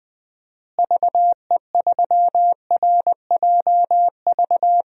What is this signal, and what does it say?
Morse code, heard as a keyed tone, about 20 wpm, 700 Hz. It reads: VE3RJV
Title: DXCC-100 Call Signs - 6 Characters - Repeated 2x - 20wpm